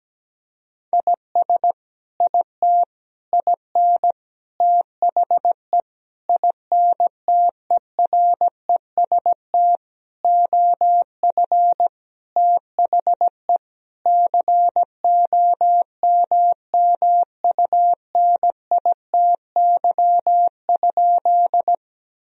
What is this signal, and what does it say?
Morse code, heard as a keyed tone, about 17 wpm, 700 Hz. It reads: IS IT IN THE INTEREST OF THE COMMUNITY?
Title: Sentences from Top 1000 Words - Review Part 4 - 17wpm